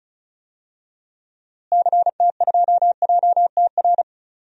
Morse code, heard as a keyed tone, about 36 wpm, 700 Hz. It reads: CT2JTR
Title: DXCC-100 Call Signs - 6 Characters - Repeated 3x - 35wpm